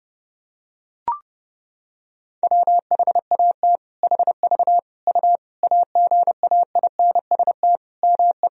Morse code, heard as a keyed tone, about 30 wpm, 700 Hz. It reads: WHAT HV U AGAINST ME
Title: Sentences from Top 400 Words and Abbreviations 30wpm